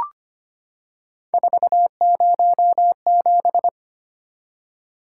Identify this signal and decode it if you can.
Morse code, heard as a keyed tone, about 25 wpm, 700 Hz. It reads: 407
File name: Three Letter-Number Combinations - 25wpm